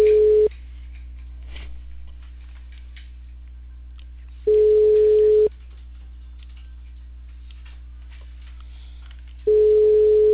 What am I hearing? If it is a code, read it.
Ringback tone (Europe, 425 Hz).